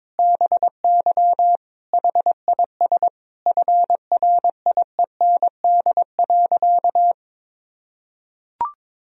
Morse code, heard as a keyed tone, about 22 wpm, 700 Hz. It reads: BY HIS FRIEND.
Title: Sentences from Top 1000 Words - Review Part 4 - 22wpm